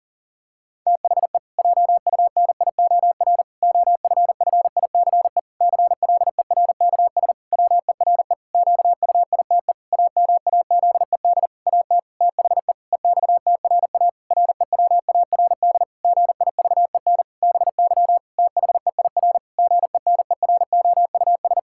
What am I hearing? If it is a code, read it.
Morse code, heard as a keyed tone, about 37 wpm, 700 Hz. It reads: THE JUNIOR OFFICE CLERKS WERE QUITE AMAZED AT THE EXTRA REWARD GIVEN BY THEIR GENEROUS
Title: Pangram Challenge v2 - 40wpm